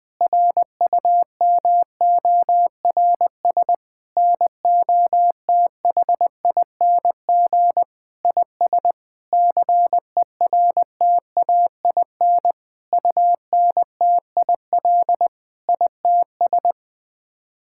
Morse code, heard as a keyed tone, about 20 wpm, 700 Hz. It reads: RUMORS NOTHING IS CERTAIN UNTIL ITS
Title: Sets of 8 Words - 20wpm